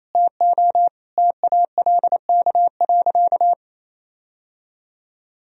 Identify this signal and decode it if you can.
Morse code, heard as a keyed tone, about 28 wpm, 700 Hz. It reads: TO TALK.
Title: Sentences from Top 1000 Words - Review Part 1 - 28wpm